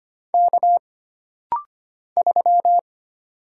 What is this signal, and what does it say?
Morse code, heard as a keyed tone, about 25 wpm, 700 Hz. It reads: K 3
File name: Single Character (Letter:Number:Punctuation:Prosign) - Rapid-Fire - 25wpm